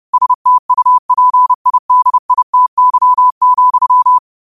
Morse code, heard as a keyed tone, about 30 wpm, 1000 Hz. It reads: STUPIDITY,
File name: qotdfiles / 30wpm